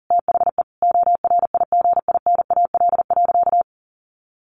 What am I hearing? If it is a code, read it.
Morse code, heard as a keyed tone, about 40 wpm, 700 Hz. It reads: THE ORIGINAL.
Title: Sentences from Top 1400 Words 40wpm